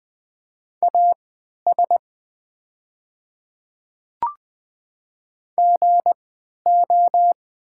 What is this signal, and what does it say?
Morse code, heard as a keyed tone, about 20 wpm, spaced as for 11 wpm, 700 Hz. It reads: AS GO
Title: 3x Character Spacing - Top 100 Words - 20wpm